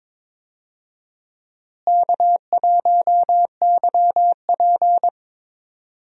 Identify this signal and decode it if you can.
Morse code, heard as a keyed tone, about 21 wpm, 700 Hz. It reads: K1YP
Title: USA Call Signs - 4 Characters - Repeated 3x - 22wpm